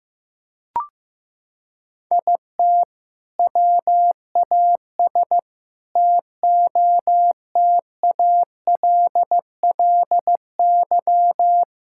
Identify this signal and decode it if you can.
Morse code, heard as a keyed tone, about 15 wpm, 700 Hz. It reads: IT WAS TOTALLY
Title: Sentences from Top 1600 Words 15wpm